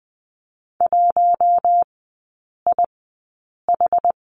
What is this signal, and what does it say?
Morse code, heard as a keyed tone, about 20 wpm, spaced as for 8 wpm, 690 Hz. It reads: 1IH
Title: 081 Lesson 14e - Number 2 - Three Characters 20wpm